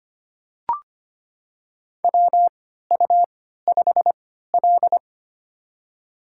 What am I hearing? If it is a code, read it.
Morse code, heard as a keyed tone, about 25 wpm, spaced as for 14 wpm, 700 Hz. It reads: WU5L
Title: USA Call Signs - 4 Characters - 3x Character Spacing - 25wpm